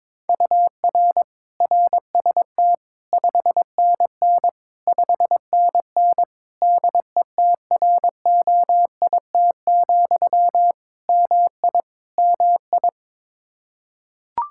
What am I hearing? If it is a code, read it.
Morse code, heard as a keyed tone, about 22 wpm, 700 Hz. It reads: UR RST 5NN 5NN DETROIT, MI MI
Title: Practice QSO - Exchange 1 - Beginning - 22wpm